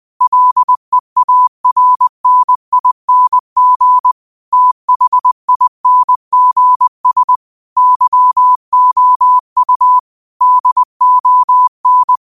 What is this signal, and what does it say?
Morse code, heard as a keyed tone, about 20 wpm, 1000 Hz. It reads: LEARNING THINGS YOU DON
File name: 20wpm